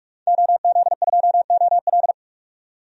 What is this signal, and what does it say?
Morse code, heard as a keyed tone, about 42 wpm, 700 Hz. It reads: OZ2OL
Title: DXCC-100 Call Signs - 5 Characters - 45wpm